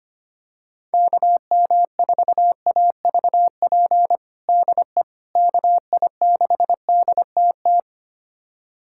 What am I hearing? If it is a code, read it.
Morse code, heard as a keyed tone, about 25 wpm, 700 Hz. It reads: KM4AVP DE KI6DTT
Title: USA Callsign DE Callsign - 6 Characters - 25wpm